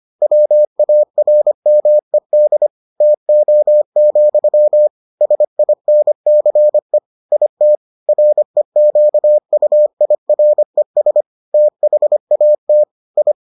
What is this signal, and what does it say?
Morse code, heard as a keyed tone, about 25 wpm, 600 Hz. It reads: WARMED TO, SINCE IT REQUIRES THAT I